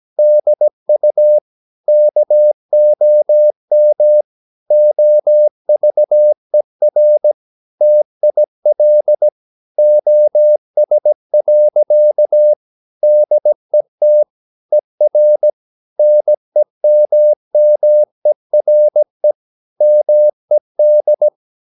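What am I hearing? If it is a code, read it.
Morse code, heard as a keyed tone, about 17 wpm, 600 Hz. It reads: DU KOM OVER TIL OS. DET ER NEMMERE MED
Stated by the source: Slow (QRS) Morse practice at 17 wpm, from CW med Gnister.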